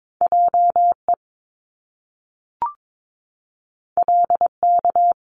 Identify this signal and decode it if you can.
Morse code, heard as a keyed tone, about 22 wpm, 700 Hz. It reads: JE LK
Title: Top Two Letter Combinations in Words 22wpm